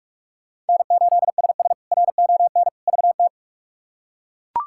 Morse code, heard as a keyed tone, about 42 wpm, 700 Hz. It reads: N8SS RON VT
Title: K1USN SST QSO Practice - 45wpm